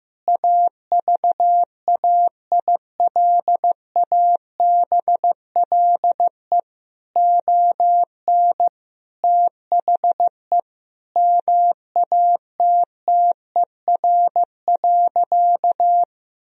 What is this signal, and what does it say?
Morse code, heard as a keyed tone, about 16 wpm, 700 Hz. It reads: AVAILABLE ON THE MATTER.
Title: Sentences from Top 1000 Words - Review Part 3 - 15wpm